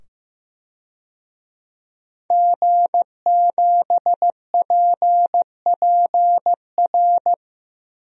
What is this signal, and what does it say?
Morse code, heard as a keyed tone, about 15 wpm, 700 Hz. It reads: G7PPR
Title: DXCC-100 Call Signs - 5 Characters - 15wpm